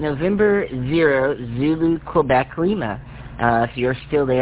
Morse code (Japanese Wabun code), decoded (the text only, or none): none